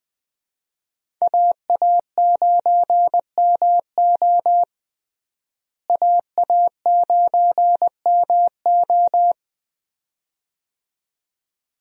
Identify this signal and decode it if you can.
Morse code, heard as a keyed tone, about 20 wpm, 700 Hz. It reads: AA9MO AA9MO
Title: USA Call Signs - 5 Characters - Repeated 2x - 20wpm